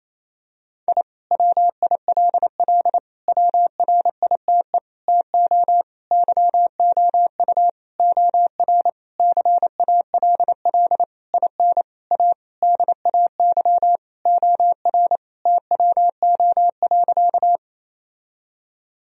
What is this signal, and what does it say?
Morse code, heard as a keyed tone, about 28 wpm, 700 Hz. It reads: I WILL WRITE TO YOU OR CALL IN A DAY OR TWO.
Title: Sentences from Top 1000 Words - Review Part 1 - 28wpm